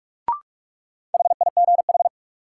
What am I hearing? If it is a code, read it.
Morse code, heard as a keyed tone, about 47 wpm, 700 Hz. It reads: HIGH
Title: Sets of 2 Words - New Word Drill - 45wpm